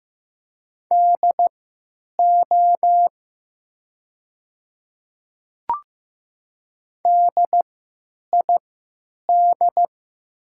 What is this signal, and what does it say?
Morse code, heard as a keyed tone, about 15 wpm, spaced as for 9 wpm, 700 Hz. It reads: DO DID
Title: 3x Character Spacing - Top 100 Words - 15wpm